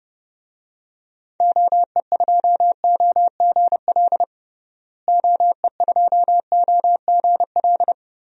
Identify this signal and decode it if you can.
Morse code, heard as a keyed tone, about 32 wpm, 700 Hz. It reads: OE2OGL OE2OGL
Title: DXCC-100 Call Signs - 6 Characters - Repeated 3x - 30wpm